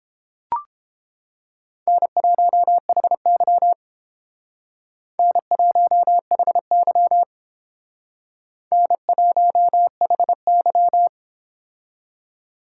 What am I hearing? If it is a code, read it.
Morse code, heard as a keyed tone, about 30 wpm, 700 Hz. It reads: N1HY N1HY N1HY